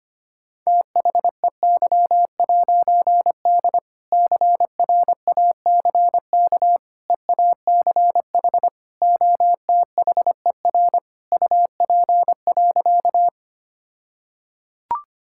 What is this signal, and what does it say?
Morse code, heard as a keyed tone, about 25 wpm, 700 Hz. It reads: THEY'D CRACK EACH OTHER UP.
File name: practice / 25wpm / All-Age Appropriate Jokes - 25wpm